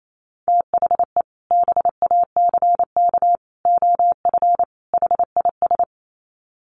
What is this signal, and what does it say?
Morse code, heard as a keyed tone, about 28 wpm, 700 Hz. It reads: THE BACK OF HIS
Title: Sets of 4 Words 28wpm